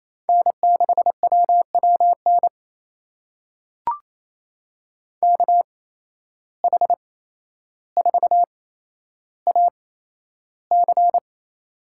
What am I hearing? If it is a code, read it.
Morse code, heard as a keyed tone, about 28 wpm, spaced as for 12 wpm, 700 Hz. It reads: N6WWN KH4AC
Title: USA Call Signs - 5 Characters - 8x Character Spacing - 28wpm